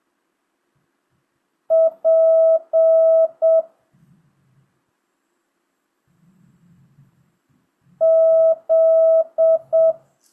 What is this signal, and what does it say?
Morse code, heard as a keyed tone, about 7 wpm, 650 Hz. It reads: P Z
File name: jh8cbh / 2025cw